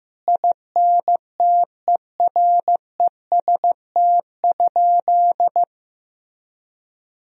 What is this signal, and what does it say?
Morse code, heard as a keyed tone, about 15 wpm, 700 Hz. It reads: INTEREST?
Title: Sentences from Top 1000 Words - Review Part 2 - 15wpm